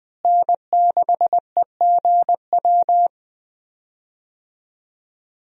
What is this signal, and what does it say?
Morse code, heard as a keyed tone, about 20 wpm, 700 Hz. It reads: N6EGW
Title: USA Call Signs - 5 Characters - Repeated 3x - 20wpm